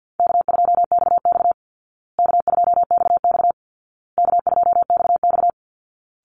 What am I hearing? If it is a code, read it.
Morse code, heard as a keyed tone, about 50 wpm, 700 Hz. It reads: K2XX K2XX K2XX
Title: Call Signs Repeated 2x - 50wpm